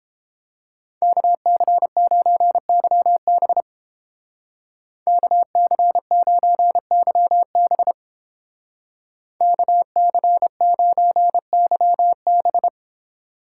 Morse code, heard as a keyed tone, about 29 wpm, 700 Hz. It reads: KC9YB KC9YB KC9YB